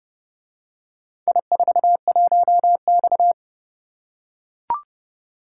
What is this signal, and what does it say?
Morse code, heard as a keyed tone, about 27 wpm, 700 Hz. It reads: I41X